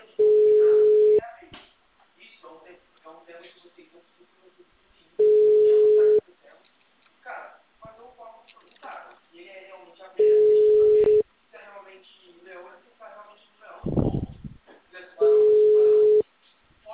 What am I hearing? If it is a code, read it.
Ringback tone (Europe, 425 Hz), with speech.
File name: out- -2011- - - .30